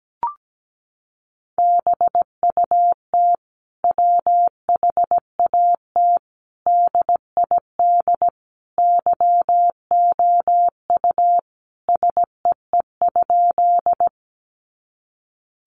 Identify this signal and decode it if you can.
Morse code, heard as a keyed tone, about 17 wpm, 700 Hz. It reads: BUT WHAT DID YOU SEE?
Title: Sentences from Top 100 Words 17wpm